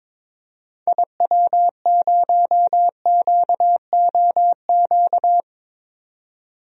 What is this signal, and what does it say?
Morse code, heard as a keyed tone, about 22 wpm, 700 Hz. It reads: IW0QOQ
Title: DXCC-100 Call Signs - 6 Characters - 22wpm